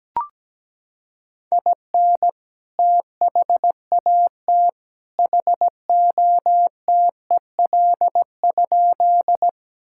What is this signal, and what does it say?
Morse code, heard as a keyed tone, about 17 wpm, 700 Hz. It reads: IN THAT HOTEL?
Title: Sentences from Top 1600 Words 17wpm